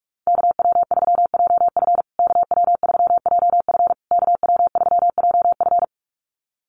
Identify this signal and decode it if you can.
Morse code, heard as a keyed tone, about 44 wpm, 700 Hz. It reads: KW3JF KW3JF KW3JF
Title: USA Call Signs Repeated 3x 45wpm